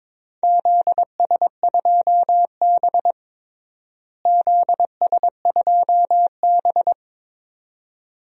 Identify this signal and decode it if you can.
Morse code, heard as a keyed tone, about 23 wpm, 700 Hz. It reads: ZS2B ZS2B
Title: DXCC-100 Call Signs - 4 Characters - Repeated 2x - 22wpm